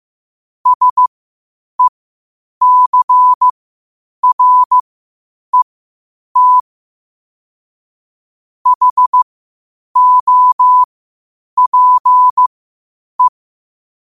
Morse code, heard as a keyed tone, about 15 wpm, spaced as for 8 wpm, 1000 Hz. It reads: SECRET HOPE
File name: qotdfiles / 8wpm